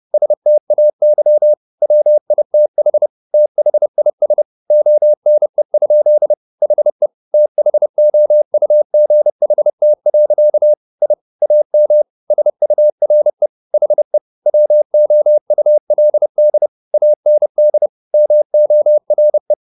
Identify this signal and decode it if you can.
Morse code, heard as a keyed tone, about 30 wpm, 600 Hz. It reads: STAY WITH THIS ONE? HE THOUGHT. I AM SURE HE WOULD AND MORE
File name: the old man and the sea / the old man and the sea 1 30wpm